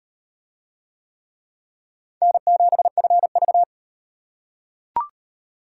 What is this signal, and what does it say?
Morse code, heard as a keyed tone, about 37 wpm, 700 Hz. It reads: N7FV